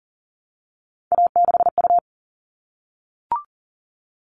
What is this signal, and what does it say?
Morse code, heard as a keyed tone, about 40 wpm, 700 Hz. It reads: A6U
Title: All Cut Numbers - 3-Digit 40wpm